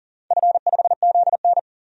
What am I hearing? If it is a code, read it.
Morse code, heard as a keyed tone, about 40 wpm, 700 Hz. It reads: F5ZN
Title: DXCC-100 Call Signs - 4 Characters - 40wpm